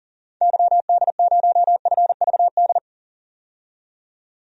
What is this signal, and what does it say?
Morse code, heard as a keyed tone, about 38 wpm, 700 Hz. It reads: YD0FVD
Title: DXCC-100 Call Signs - 6 Characters - 40wpm